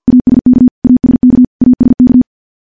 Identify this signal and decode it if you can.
Morse code, heard as a keyed tone, about 50 wpm, 250 Hz. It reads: ASK ASK ASK
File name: morse / morse-training-50WPM